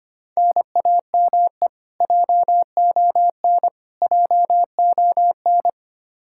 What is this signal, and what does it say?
Morse code, heard as a keyed tone, about 25 wpm, 700 Hz. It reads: NAME JON JON
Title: K1USN SST QSO Practice 25wpm